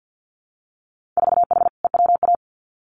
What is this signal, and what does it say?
Morse code, heard as a keyed tone, about 50 wpm, 700 Hz. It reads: 4H EWA
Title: Winter Field Day Exchanges Encore 50wpm